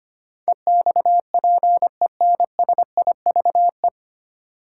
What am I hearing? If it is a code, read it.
Morse code, heard as a keyed tone, about 25 wpm, 700 Hz. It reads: EXPENSIVE
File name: practice / 25wpm / 9 Letter Words - First 3 Letters Given - 25wpm